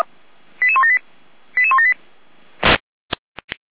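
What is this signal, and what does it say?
Five-tone selective call (ZVEI), heard as quick runs of five tones, tones of about 70 ms. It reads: 88178 88178
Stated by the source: ZVEI five-tone selective call 88179, received by SDR.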